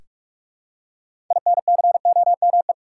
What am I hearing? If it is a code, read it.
Morse code, heard as a keyed tone, about 46 wpm, 700 Hz. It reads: INCOME
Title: Six Letter Words - 45wpm